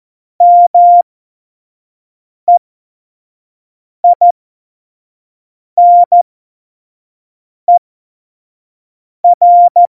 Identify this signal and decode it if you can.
Morse code, heard as a keyed tone, about 14 wpm, spaced as for 5 wpm, 700 Hz. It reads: MEINER
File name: Morsekurs / Witze / Witz 26. November 2019, 25 F70